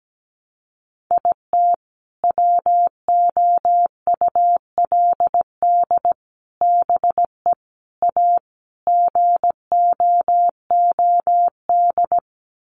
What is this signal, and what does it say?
Morse code, heard as a keyed tone, about 17 wpm, 700 Hz. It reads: IT WOULD BE A GOOD
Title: Sets of 5 Words Encore2 17wpm